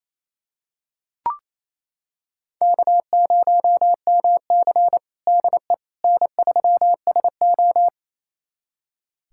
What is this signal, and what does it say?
Morse code, heard as a keyed tone, about 28 wpm, 700 Hz. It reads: K0MC DE N3SO
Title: USA Callsign DE Callsign - 4 Characters - 28wpm